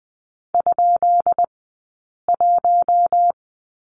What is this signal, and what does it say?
Morse code, heard as a keyed tone, about 20 wpm, spaced as for 8 wpm, 700 Hz. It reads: ?1
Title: 211 Lesson 34d - Number 8 - Two Characters 20wpm